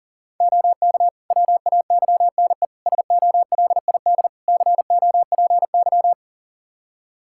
Morse code, heard as a keyed tone, about 40 wpm, 700 Hz. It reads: OK WAYNE SOLID COPY
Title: Practice QSO - Exchange 3 - More Info - 40wpm